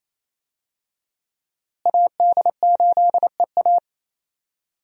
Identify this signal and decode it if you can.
Morse code, heard as a keyed tone, about 28 wpm, 700 Hz. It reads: AD8EA
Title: USA Call Signs - 5 Characters - Repeated 3x - 28wpm